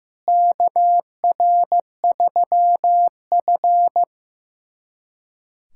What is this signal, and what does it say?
Morse code, heard as a keyed tone, about 15 wpm, 700 Hz. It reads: KR3F
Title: USA Callsign DE Callsign - 4 Characters - 15wpm